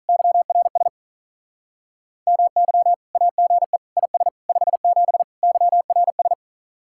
Morse code, heard as a keyed tone, about 45 wpm, 700 Hz. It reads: YRS MY AGE IS 57 YRS